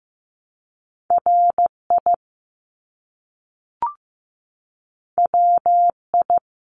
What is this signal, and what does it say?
Morse code, heard as a keyed tone, about 15 wpm, 700 Hz. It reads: RI WI
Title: US States Repeated 3x 15wpm